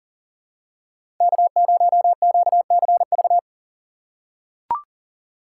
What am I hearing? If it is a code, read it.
Morse code, heard as a keyed tone, about 40 wpm, 690 Hz. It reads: K0QCV
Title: USA Call Signs - 5 Characters - 3x Character Spacing - 40wpm